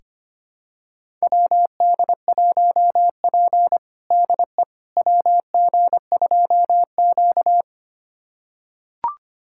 Morse code, heard as a keyed tone, about 25 wpm, 700 Hz. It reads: WD1P DE WG2Q
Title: USA Callsign DE Callsign - 4 Characters - 25wpm